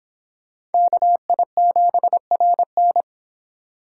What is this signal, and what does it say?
Morse code, heard as a keyed tone, about 30 wpm, 700 Hz. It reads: KI7RN